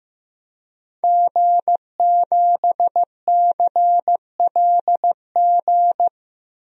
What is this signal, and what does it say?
Morse code, heard as a keyed tone, about 15 wpm, 700 Hz. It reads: G7CLG
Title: DXCC-100 Call Signs - 5 Characters - 15wpm